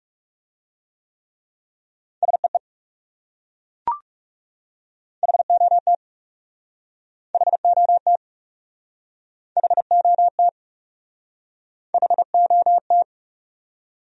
Morse code, heard as a keyed tone, about 38 wpm, 700 Hz. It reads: SEE HOT HOT HOT HOT